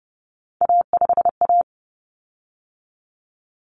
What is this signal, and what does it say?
Morse code, heard as a keyed tone, about 30 wpm, 700 Hz. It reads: A5A
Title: Common Cut Numbers - 3-Digit 30wpm